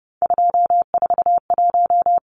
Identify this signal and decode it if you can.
Morse code, heard as a keyed tone, about 30 wpm, 700 Hz. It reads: 241
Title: Numbers - 3-Digit 30wpm